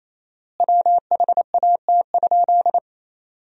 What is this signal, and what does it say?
Morse code, heard as a keyed tone, about 28 wpm, 700 Hz. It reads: WHAT?
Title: 4x Word-Spacing - Sentences from Top 100 Words - 28wpm